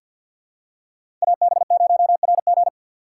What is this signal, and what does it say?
Morse code, heard as a keyed tone, about 53 wpm, 700 Hz. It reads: AB0RG
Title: USA Call Signs - 5 Characters - 6x Character Spacing - 50wpm